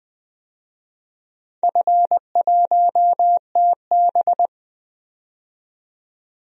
Morse code, heard as a keyed tone, about 20 wpm, 700 Hz. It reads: F1TB
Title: DXCC-100 Call Signs - 4 Characters - 20wpm